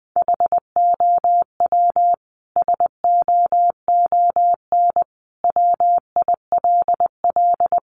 Morse code, heard as a keyed tone, about 19 wpm, 700 Hz. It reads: HOW SOON WILL
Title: Sentences from Top 400 Words 20wpm